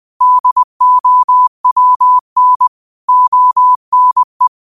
Morse code, heard as a keyed tone, about 20 wpm, 1000 Hz. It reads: DOWN ONE